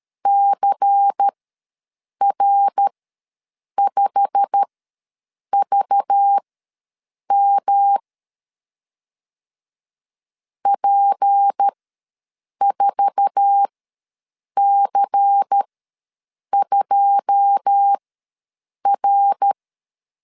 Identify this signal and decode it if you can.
Morse code, heard as a keyed tone, about 13 wpm, spaced as for 7 wpm, 800 Hz. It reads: CR5VM P4C2R
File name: LEZIONI / 9 2 ws 12wpm 3406